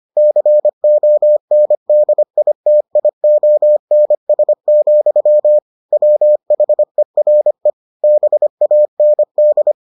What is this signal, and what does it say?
Morse code, heard as a keyed tone, about 25 wpm, 600 Hz. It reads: CONDITIONS, WHERE BAND